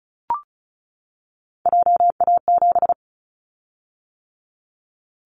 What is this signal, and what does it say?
Morse code, heard as a keyed tone, about 36 wpm, 700 Hz. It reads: JA7
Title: Callsign Prefixes for Top 50 DXCC - 35wpm